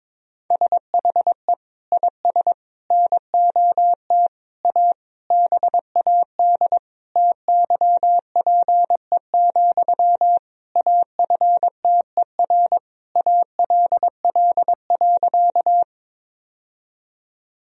Morse code, heard as a keyed tone, about 22 wpm, 700 Hz. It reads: SHE IS NOT A BAD TYPE, AFTER ALL.
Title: Sentences from Top 1000 Words - Review Part 3 - 22wpm